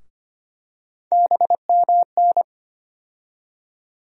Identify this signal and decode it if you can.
Morse code, heard as a keyed tone, about 25 wpm, 700 Hz. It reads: BMN
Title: USA Call Sign Postfixes - 3 Letters 25wpm